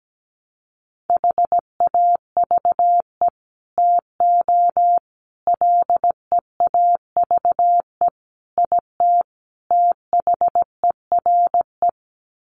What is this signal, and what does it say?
Morse code, heard as a keyed tone, about 17 wpm, 700 Hz. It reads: HAVE TO LEAVE IT THERE
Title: Sets of 5 Words Encore2 17wpm